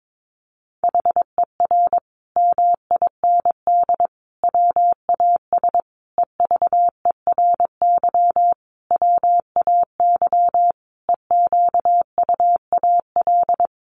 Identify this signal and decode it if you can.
Morse code, heard as a keyed tone, about 22 wpm, 700 Hz. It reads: HER MIND WAS EVERY WAY EQUAL
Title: Sentences from Top 1400 Words 22wpm